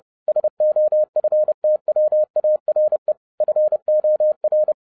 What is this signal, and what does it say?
Morse code, heard as a keyed tone, about 30 wpm, 610 Hz. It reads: SOFTWARE FOR